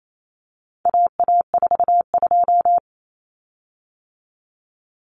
Morse code, heard as a keyed tone, about 28 wpm, 700 Hz. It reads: AA42
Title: Common Cut Numbers - 4-Digit 28wpm